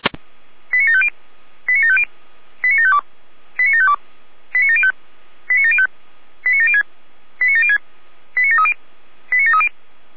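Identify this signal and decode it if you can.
Five-tone selective call (ZVEI), heard as quick runs of five tones, tones of about 80 ms. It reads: 89750 89750 89752 89752 89705 89705 89706 89706 89730 89730
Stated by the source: ZVEI selective call received by SDR.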